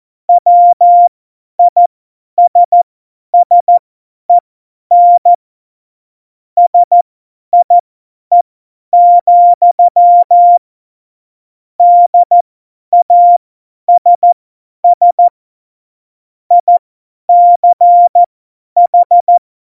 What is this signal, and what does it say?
Morse code, heard as a keyed tone, about 14 wpm, spaced as for 10 wpm, 700 Hz. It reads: WISSEN SIE, DASS ICH
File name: Morsekurs / Witze / Witz 12 Mai 2020, 50 F70